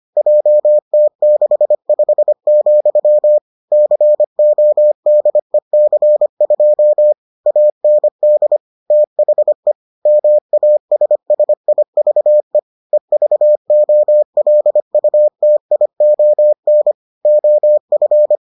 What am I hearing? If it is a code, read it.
Morse code, heard as a keyed tone, about 25 wpm, 600 Hz. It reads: JT65, CODEC2 AND THE MASSIVE EVOLUTION OF